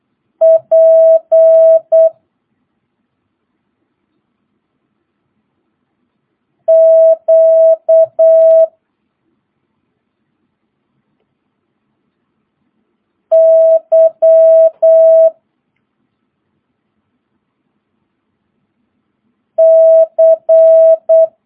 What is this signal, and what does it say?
Morse code, heard as a keyed tone, about 8 wpm, 650 Hz. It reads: P Q Y C
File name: jh8cbh / 2022cw